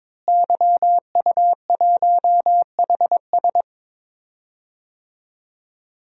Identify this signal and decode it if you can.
Morse code, heard as a keyed tone, about 22 wpm, 700 Hz. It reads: YU1HS
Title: DXCC-100 Call Signs - 5 Characters - Repeated 2x - 22wpm